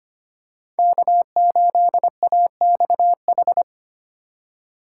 Morse code, heard as a keyed tone, about 25 wpm, 700 Hz. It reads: K8AXH
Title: USA Call Signs - 5 Characters - 4x Character Spacing - 25wpm